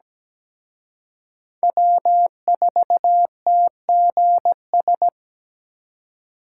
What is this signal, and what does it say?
Morse code, heard as a keyed tone, about 20 wpm, 700 Hz. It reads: W4TGS
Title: Speed Racing - USA 1x3 Call Signs 17wpm